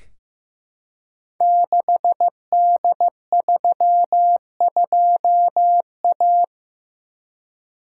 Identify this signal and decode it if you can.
Morse code, heard as a keyed tone, about 15 wpm, 700 Hz. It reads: 6D32A
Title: Code Groups - 5 Letter:Numbers - 15wpm